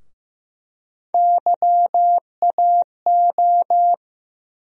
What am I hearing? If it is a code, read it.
Morse code, heard as a keyed tone, about 15 wpm, 700 Hz. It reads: YAO
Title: USA Call Sign Postfixes - 3 Letters 15wpm